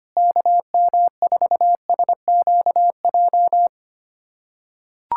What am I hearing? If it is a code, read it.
Morse code, heard as a keyed tone, about 25 wpm, 700 Hz. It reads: KM4SQJ
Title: USA Callsign DE Callsign - 6 Characters - 25wpm